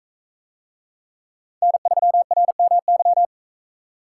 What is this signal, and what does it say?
Morse code, heard as a keyed tone, about 41 wpm, 700 Hz. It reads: N3RMY